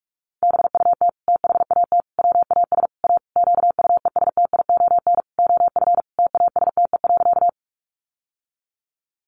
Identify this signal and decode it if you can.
Morse code, heard as a keyed tone, about 44 wpm, 700 Hz. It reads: BUT THAT WAS A QUESTION OF TASTE.
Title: Sentences from Top 1100 Words 45wpm